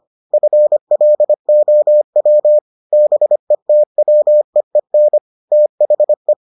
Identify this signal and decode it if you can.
Morse code, heard as a keyed tone, about 25 wpm, 600 Hz. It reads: FLOW BETWEEN THE